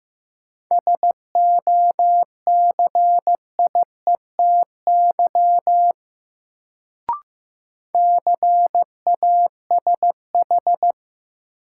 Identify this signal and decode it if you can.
Morse code, heard as a keyed tone, about 15 wpm, 700 Hz. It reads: SOCIETY CASH
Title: Sentences from Top 700 Words - New Word Drill - 15wpm